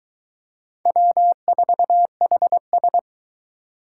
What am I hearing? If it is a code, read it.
Morse code, heard as a keyed tone, about 22 wpm, 700 Hz. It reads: W4HS